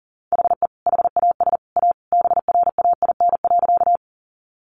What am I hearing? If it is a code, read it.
Morse code, heard as a keyed tone, about 40 wpm, 700 Hz. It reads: HE HAS A BRAIN.
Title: Sentences from Top 1100 Words 40wpm